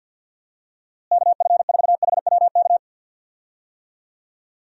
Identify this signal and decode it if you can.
Morse code, heard as a keyed tone, about 50 wpm, 700 Hz. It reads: KF4HWK
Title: USA Call Signs - 6 Characters - Repeated 2x - 50wpm